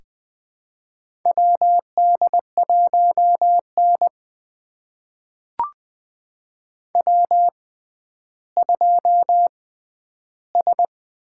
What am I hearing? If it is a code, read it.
Morse code, heard as a keyed tone, about 21 wpm, spaced as for 11 wpm, 700 Hz. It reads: WD1N W2S
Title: USA Call Signs - 4 Characters - 6x Character Spacing - 20wpm